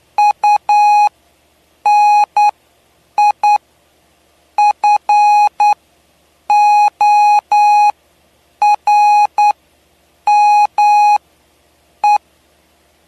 Morse code, heard as a keyed tone, about 9 wpm, spaced as for 7 wpm, 830 Hz. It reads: UNIFORME
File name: cw / py2lcd / 07-PY2LCD-Li��o 7